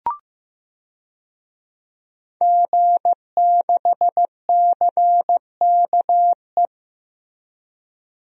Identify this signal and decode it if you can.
Morse code, heard as a keyed tone, about 15 wpm, 700 Hz. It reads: G6CKE
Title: DXCC-100 Call Signs - 5 Characters - Repeated 2x - 15wpm